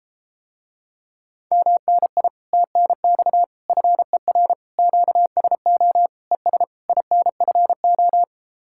Morse code, heard as a keyed tone, about 34 wpm, 700 Hz. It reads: MNI TNX FER QSO ES INFO